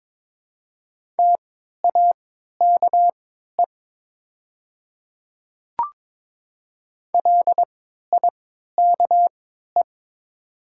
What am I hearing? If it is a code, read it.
Morse code, heard as a keyed tone, about 22 wpm, spaced as for 13 wpm, 700 Hz. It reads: TAKE LIKE
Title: 3x Character Spacing - Top 200 Words - 22wpm